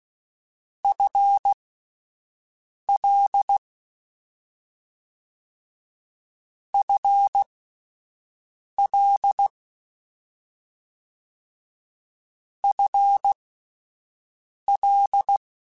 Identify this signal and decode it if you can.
Morse code, heard as a keyed tone, about 16 wpm, spaced as for 5 wpm, 800 Hz. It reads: FL FL FL
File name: Combinations / Combinations-mp3s / Combinations@5